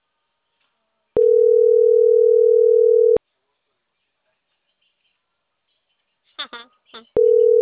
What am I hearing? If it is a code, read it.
Ringback tone (North America), 440 + 480 Hz, with speech.
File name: CENTRAL CAEA / out agcaea10